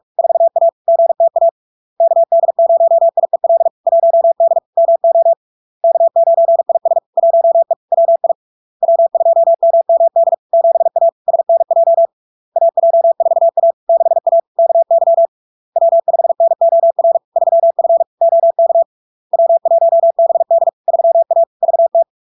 Morse code, heard as a keyed tone, about 46 wpm, 670 Hz. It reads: 4A GTA KD0IEL 1D MO K9IS 1E WI W2MMD 7A SNJ AJ4A 6A KY W5NOR 3F OK W1BD 3A VT
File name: LARK-FD-2022-Log-in-CW-45wpm-670-Hz